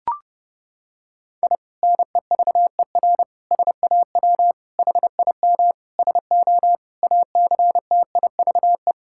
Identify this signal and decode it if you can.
Morse code, heard as a keyed tone, about 30 wpm, 700 Hz. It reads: I NEVER SAW HIM SO ACTIVE
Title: Sentences from Top 1000 Words - Review Part 6 - 30wpm